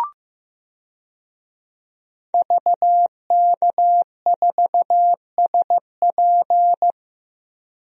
Morse code, heard as a keyed tone, about 15 wpm, 700 Hz. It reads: VK4SP
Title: DXCC-100 Call Signs - 5 Characters - 15wpm